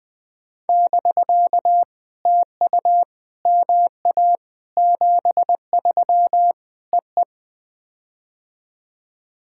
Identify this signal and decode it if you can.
Morse code, heard as a keyed tone, about 20 wpm, 700 Hz. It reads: <BK> TU MA 73 EE
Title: POTA QSO Practice - 20wpm